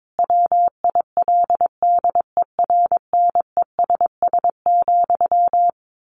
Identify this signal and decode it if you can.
Morse code, heard as a keyed tone, about 22 wpm, 700 Hz. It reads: WILDERNESS,
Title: A Sound Of Thunder 22wpm